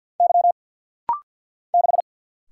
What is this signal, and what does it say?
Morse code, heard as a keyed tone, about 49 wpm, 690 Hz. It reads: Y 6
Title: Single Letter-Number Alt - Rapid-Fire 50wpm